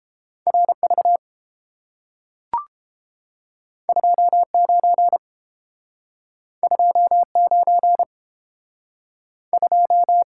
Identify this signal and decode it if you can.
Morse code, heard as a keyed tone, about 30 wpm, 700 Hz. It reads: RV 29 29 2